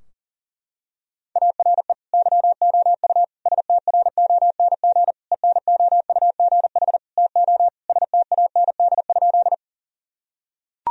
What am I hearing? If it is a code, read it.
Morse code, heard as a keyed tone, about 40 wpm, 700 Hz. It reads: ARE YOU STRONG ENOUGH TO STAND?
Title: Sentences from Top 1000 Words - Review Part 3 - 40wpm